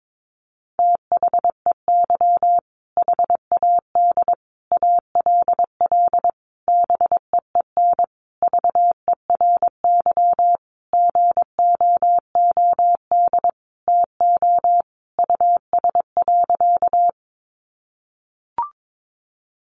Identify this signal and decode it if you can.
Morse code, heard as a keyed tone, about 22 wpm, 700 Hz. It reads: THEY HAD ALL BEEN VERY GOOD TO US.
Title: Sentences from Top 200 Words 22wpm